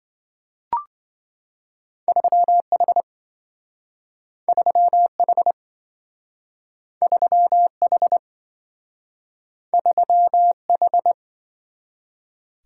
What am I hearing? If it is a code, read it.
Morse code, heard as a keyed tone, about 27 wpm, 700 Hz. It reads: 3H 3H 3H 3H